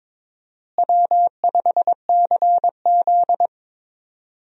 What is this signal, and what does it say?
Morse code, heard as a keyed tone, about 22 wpm, 700 Hz. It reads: W5CZ
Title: USA Call Signs - 4 Characters - Repeated 2x - 22wpm